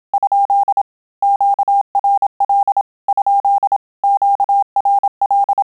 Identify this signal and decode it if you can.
Morse code, heard as a keyed tone, about 27 wpm, 800 Hz. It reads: ? QRL ? QRL
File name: Telegraphy-group / QSO CW 25 PPM